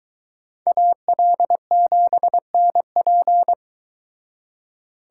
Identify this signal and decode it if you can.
Morse code, heard as a keyed tone, about 22 wpm, 700 Hz. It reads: AL7NP